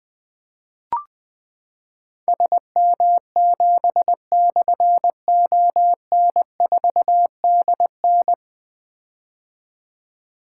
Morse code, heard as a keyed tone, about 21 wpm, 700 Hz. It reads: SM7/ON4DN
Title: Difficult Call Signs 20wpm